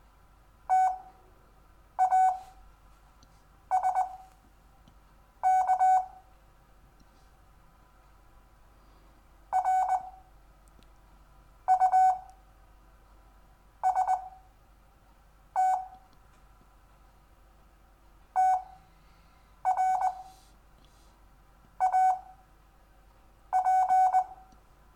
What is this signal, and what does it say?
Morse code, heard as a keyed tone, about 20 wpm, spaced as for 5 wpm, 750 Hz. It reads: TASK RUST TRAP
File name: cw / Lesson4 REVIEW